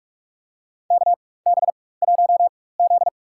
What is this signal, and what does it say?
Morse code, heard as a keyed tone, about 44 wpm, spaced as for 21 wpm, 700 Hz. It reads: KB1Z
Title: USA Call Signs - 4 Characters - 4x Character Spacing - 45wpm